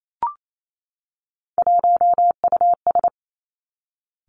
Morse code, heard as a keyed tone, about 28 wpm, 700 Hz. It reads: 1US
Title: Three Challenging Letter-Number Combinations 28wpm